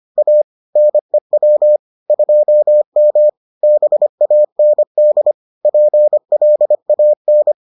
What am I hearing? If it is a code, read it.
Morse code, heard as a keyed tone, about 25 wpm, 600 Hz. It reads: A NEW 2M BAND PLAN